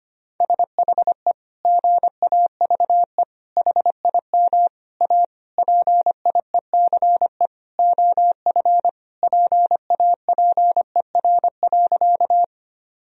Morse code, heard as a keyed tone, about 25 wpm, 700 Hz. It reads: SHE GAVE HIM A PIECE OF PAPER.
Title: Sentences from Top 1000 Words - Review Part 2 - 25wpm